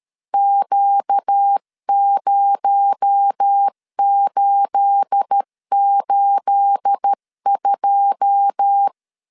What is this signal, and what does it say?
Morse code, heard as a keyed tone, about 13 wpm, 800 Hz. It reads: Q0882
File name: LEZIONI / 9 28 12wpm 2351